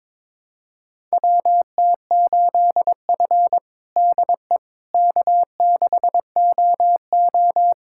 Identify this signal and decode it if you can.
Morse code, heard as a keyed tone, about 22 wpm, 700 Hz. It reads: WT8F DE K6OO
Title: USA Callsign DE Callsign - 4 Characters - 22wpm